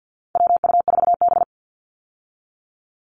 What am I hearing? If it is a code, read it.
Morse code, heard as a keyed tone, about 50 wpm, 700 Hz. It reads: WU4B
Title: Call Signs Repeated 2x - 50wpm